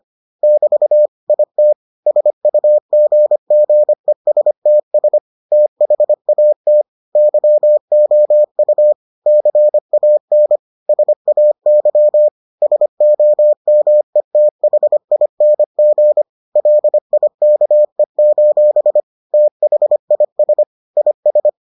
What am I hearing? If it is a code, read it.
Morse code, heard as a keyed tone, about 25 wpm, 600 Hz. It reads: <BT> IT SUGGESTS THAT YOU CAN SAY SOMETHING LIKE: THIS IS